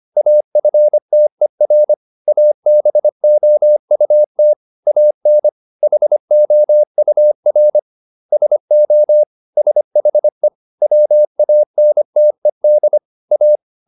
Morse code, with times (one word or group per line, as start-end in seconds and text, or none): 0.2-1.9 AFTER
2.3-4.5 ABOUT
4.9-5.5 AN
5.8-7.8 HOUR
8.3-9.2 SO
9.6-10.5 SHE
10.8-13.0 WANTED
13.3-13.6 A